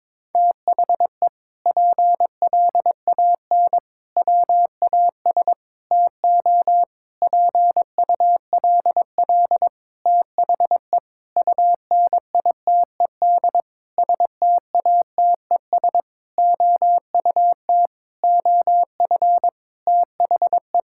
Morse code, heard as a keyed tone, about 22 wpm, 700 Hz. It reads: THE PLAN WAS TO PULL THE UNITED STATES OUT OF THE
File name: practice / 22wpm / Sentences with 8-Word Phrases - 22wpm